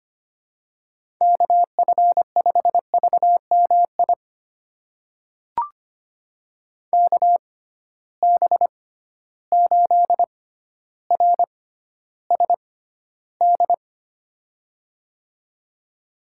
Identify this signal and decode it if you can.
Morse code, heard as a keyed tone, about 25 wpm, spaced as for 13 wpm, 700 Hz. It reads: KF5VMI KB8RSD
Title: USA Call Signs - 6 Characters - 6x Character Spacing - 25wpm